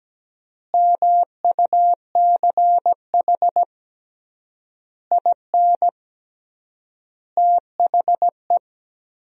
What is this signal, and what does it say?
Morse code, heard as a keyed tone, about 17 wpm, 700 Hz. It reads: MUCH IN THE
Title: 3x Word-Spacing - Sentences from Top 200 Words - 17wpm